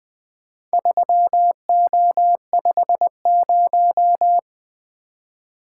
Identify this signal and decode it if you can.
Morse code, heard as a keyed tone, about 20 wpm, 700 Hz. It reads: 3O50
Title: Four Letter-Number Combinations - 20wpm